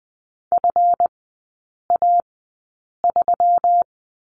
Morse code, heard as a keyed tone, about 20 wpm, spaced as for 8 wpm, 700 Hz. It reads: FA3
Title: 138 Lesson 23e - Letter f - Three Characters 20wpm